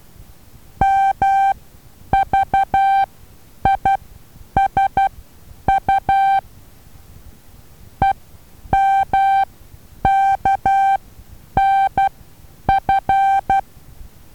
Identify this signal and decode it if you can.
Morse code, heard as a keyed tone, about 12 wpm, spaced as for 9 wpm, 780 Hz. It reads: MVISU EMKNF
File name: Taviro HA6VA B